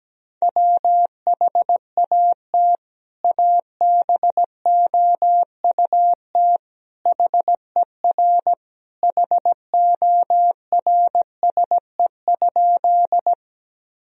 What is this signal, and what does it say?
Morse code, heard as a keyed tone, about 17 wpm, 700 Hz. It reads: WHAT ABOUT HER HORSE?
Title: Sentences from Top 1000 Words - Review Part 1 - 17wpm